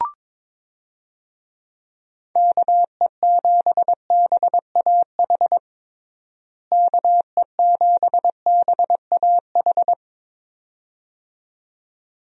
Morse code, heard as a keyed tone, about 22 wpm, 700 Hz. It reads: KE7BAH KE7BAH
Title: USA Call Signs - 6 Characters - Repeated 2x - 22wpm